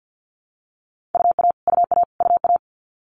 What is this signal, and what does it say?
Morse code, heard as a keyed tone, about 50 wpm, 700 Hz. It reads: UA UA UA
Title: Multi-letter Phonemes Repeated 3x 50wpm